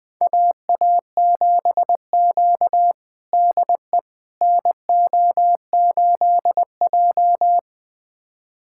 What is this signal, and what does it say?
Morse code, heard as a keyed tone, about 21 wpm, 700 Hz. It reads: AA7Q DE NO8J
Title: USA Callsign DE Callsign - 4 Characters - 20wpm